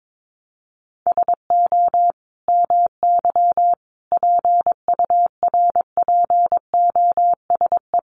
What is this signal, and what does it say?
Morse code, heard as a keyed tone, about 22 wpm, 700 Hz. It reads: SO MY PURPOSE
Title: Sentences from Top 1100 Words 22wpm